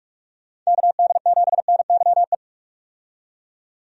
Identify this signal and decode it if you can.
Morse code, heard as a keyed tone, about 41 wpm, 700 Hz. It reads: KD7NYE
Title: USA Call Signs - 6 Characters - 45wpm